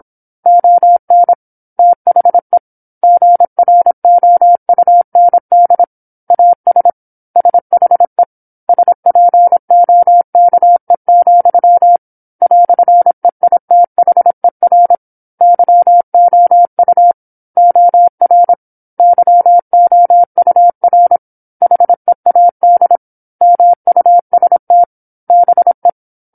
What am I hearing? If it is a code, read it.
Morse code, heard as a keyed tone, about 27 wpm, 700 Hz. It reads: ON THE GROUND AS SHE SPOKE, "EITHER YOU OR YOUR HEAD MUST BE
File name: 08-Chapter-08 26 WPM